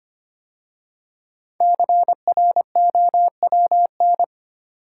Morse code, heard as a keyed tone, about 25 wpm, 700 Hz. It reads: CROWN
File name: 5 Letter Words - First 3 Letters Given - 25wpm